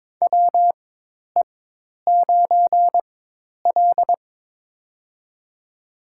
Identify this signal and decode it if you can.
Morse code, heard as a keyed tone, about 22 wpm, spaced as for 10 wpm, 700 Hz. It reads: WE9L
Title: USA Call Signs - 4 Characters - 4x Character Spacing - 22wpm